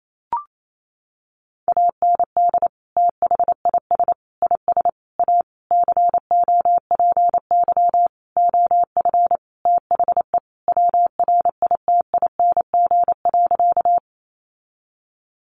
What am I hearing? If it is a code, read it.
Morse code, heard as a keyed tone, about 28 wpm, 700 Hz. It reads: AND THIS IS A COPY OF THE WRITING.
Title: Sentences from Top 1400 Words 28wpm